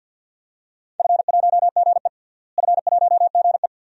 Morse code, heard as a keyed tone, about 51 wpm, 700 Hz. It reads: F1GE F1GE
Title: DXCC-100 Call Signs - 4 Characters - Repeated 2x - 50wpm